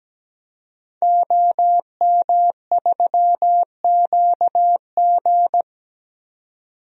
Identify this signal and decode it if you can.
Morse code, heard as a keyed tone, about 17 wpm, 700 Hz. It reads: OM3QG
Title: DXCC-100 Call Signs - 5 Characters - 17wpm